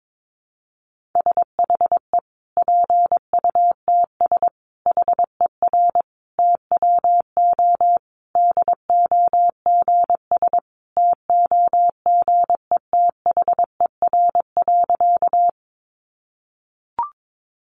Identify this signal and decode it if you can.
Morse code, heard as a keyed tone, about 22 wpm, 700 Hz. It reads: SHE PUTS HER TWO DOGS TOGETHER.
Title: Easy Phrases 22wpm